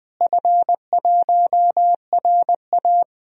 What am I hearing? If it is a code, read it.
Morse code, heard as a keyed tone, about 20 wpm, 700 Hz. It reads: F1RA
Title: DXCC-100 Call Signs - 4 Characters - Repeated 3x - 20wpm